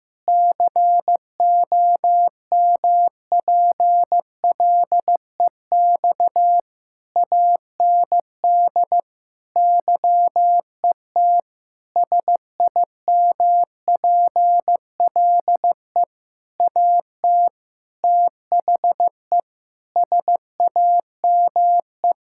Morse code, heard as a keyed tone, about 15 wpm, 700 Hz. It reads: COMPLEX AND YET SIMPLE AT THE SAME
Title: Sentences from Top 1000 Words - Review Part 6 - 15wpm